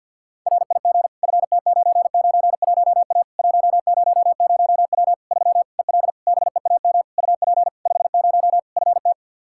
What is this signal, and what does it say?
Morse code, heard as a keyed tone, about 50 wpm, 700 Hz. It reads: RIG FT991A 100W 3 EL BEAM UP 50 FT
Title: K1USN SST QSO Practice 50wpm